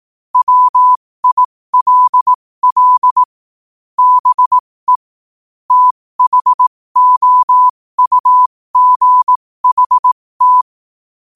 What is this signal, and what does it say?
Morse code, heard as a keyed tone, about 18 wpm, spaced as for 15 wpm, 1000 Hz. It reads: WILL BE THOUGHT
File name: qotdfiles / 15wpm